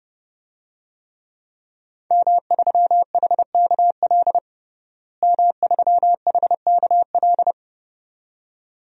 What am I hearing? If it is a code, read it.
Morse code, heard as a keyed tone, about 30 wpm, 700 Hz. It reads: M3HKL M3HKL
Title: DXCC-100 Call Signs - 5 Characters - Repeated 2x - 30wpm